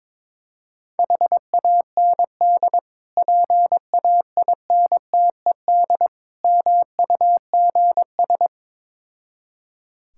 Morse code, heard as a keyed tone, about 22 wpm, 700 Hz. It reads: HAND PAINTED MUGS
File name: Difficult Words with Hints - 22wpm